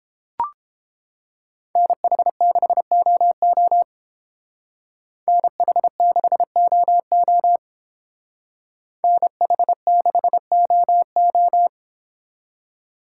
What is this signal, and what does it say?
Morse code, heard as a keyed tone, about 28 wpm, 700 Hz. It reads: NH6OO NH6OO NH6OO